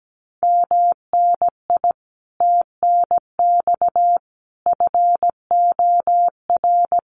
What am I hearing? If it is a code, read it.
Morse code, heard as a keyed tone, about 17 wpm, 700 Hz. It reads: MNI TNX FOR
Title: Common QSO Phrases 17wpm